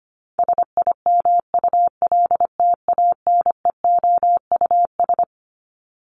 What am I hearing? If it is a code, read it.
Morse code, heard as a keyed tone, about 25 wpm, 700 Hz. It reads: SIMULTANEOUS
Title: LY Suffix - Top 100 Words with Base Words 25wpm